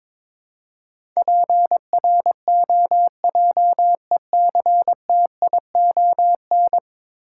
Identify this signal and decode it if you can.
Morse code, heard as a keyed tone, about 22 wpm, 700 Hz. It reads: PROJECTION
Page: morse-code-ninja-practice-22wpm